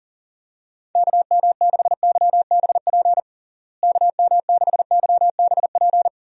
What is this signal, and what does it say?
Morse code, heard as a keyed tone, about 37 wpm, 700 Hz. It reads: KM6YBP KM6YBP
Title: USA Call Signs - 6 Characters - Repeated 3x - 40wpm